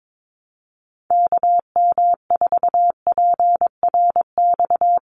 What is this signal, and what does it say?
Morse code, heard as a keyed tone, about 22 wpm, 700 Hz. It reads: KM4PRX
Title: USA Call Signs 22wpm